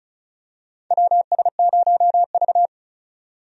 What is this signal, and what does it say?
Morse code, heard as a keyed tone, about 36 wpm, 700 Hz. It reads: WS0V
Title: USA Call Signs - 4 Characters - 35wpm